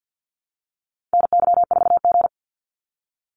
Morse code, heard as a keyed tone, about 48 wpm, 700 Hz. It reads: NY4G
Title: NA SOTA Activator Callsigns 50wpm